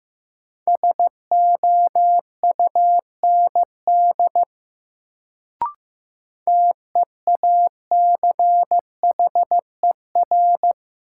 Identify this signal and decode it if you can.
Morse code, heard as a keyed tone, about 15 wpm, 700 Hz. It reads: SOUND TEACHER
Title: Sentences from Top 900 Words - New Word Drill - 15wpm